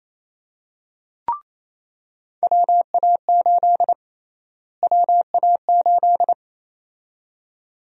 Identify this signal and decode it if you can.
Morse code, heard as a keyed tone, about 28 wpm, 700 Hz. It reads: WA8 WA8
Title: USA Call Sign Prefixes - Repeated 2x - 28wpm